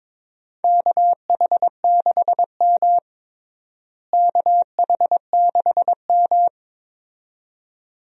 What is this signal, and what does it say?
Morse code, heard as a keyed tone, about 22 wpm, 700 Hz. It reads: KH6M KH6M
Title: USA Call Signs - 4 Characters - Repeated 2x - 22wpm